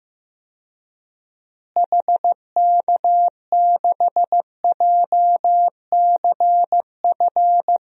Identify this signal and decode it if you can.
Morse code, heard as a keyed tone, about 15 wpm, 700 Hz. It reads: HK6JCF
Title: DXCC-100 Call Signs - 6 Characters - Repeated 2x - 15wpm